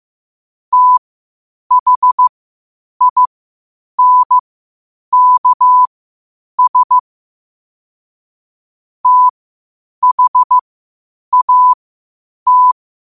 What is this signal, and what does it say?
Morse code, heard as a keyed tone, about 15 wpm, spaced as for 8 wpm, 1000 Hz. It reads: THINKS THAT